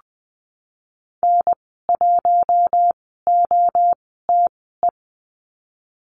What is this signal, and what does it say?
Morse code, heard as a keyed tone, about 20 wpm, spaced as for 14 wpm, 700 Hz. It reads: N1OTE
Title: 048 Lesson 9g - Number 4 - Call Signs 20wpm